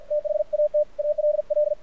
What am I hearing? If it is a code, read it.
Morse code, heard as a keyed tone, about 50 wpm, 600 Hz, with radio noise. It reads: THAT ALL
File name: archived / CRSnet